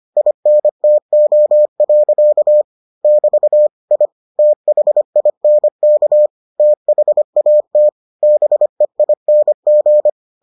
Morse code, heard as a keyed tone, about 25 wpm, 600 Hz. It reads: INTO. <BT> I THINK THAT BEING